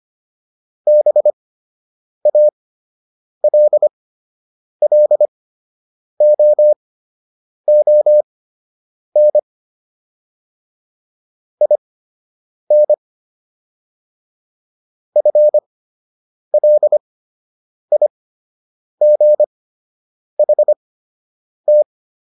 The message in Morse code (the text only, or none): BALLOON IN FLIGHT